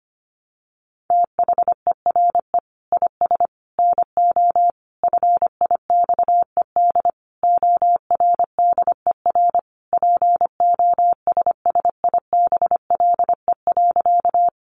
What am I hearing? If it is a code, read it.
Morse code, heard as a keyed tone, about 25 wpm, 700 Hz. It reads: THERE IS NO FIXED ORDER POSSIBLE.
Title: Sentences from Top 1100 Words 25wpm